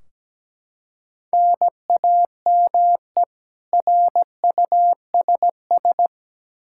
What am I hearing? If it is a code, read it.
Morse code, heard as a keyed tone, about 17 wpm, 700 Hz. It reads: NAME RUSS
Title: K1USN SST QSO Practice 17wpm